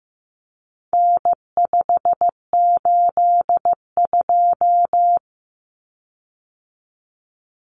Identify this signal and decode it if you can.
Morse code, heard as a keyed tone, about 15 wpm, 700 Hz. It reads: N582
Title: Common Cut Numbers - 4-Digit 15wpm